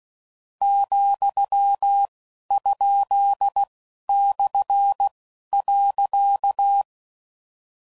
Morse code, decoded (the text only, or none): , ? / .